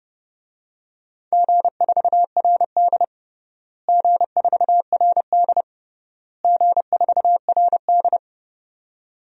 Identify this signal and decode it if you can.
Morse code, heard as a keyed tone, about 30 wpm, 700 Hz. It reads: G4RD G4RD G4RD
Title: DXCC-100 Call Signs - 4 Characters - Repeated 3x - 30wpm